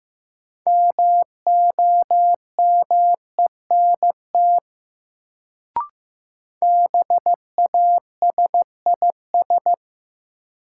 Morse code, heard as a keyed tone, about 15 wpm, 700 Hz. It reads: MOMENT BASIS
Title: Sets of 3 Words - New Word Drill - 15wpm